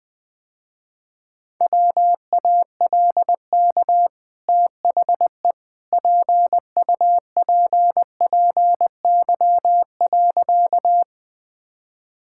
Morse code, heard as a keyed tone, about 21 wpm, 700 Hz. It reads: WALK THE PUPPY.
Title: Easy Sentences with Hints - 20wpm